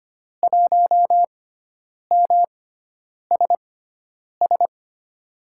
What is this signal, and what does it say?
Morse code, heard as a keyed tone, about 25 wpm, spaced as for 9 wpm, 700 Hz. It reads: 1MSS
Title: USA Call Signs - 6x Character Spacing - 25wpm